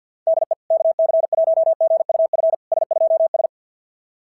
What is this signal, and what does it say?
Morse code, heard as a keyed tone, about 50 wpm, 650 Hz. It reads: DE KC1GUF S2S
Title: SOTA QSO Practice 50wpm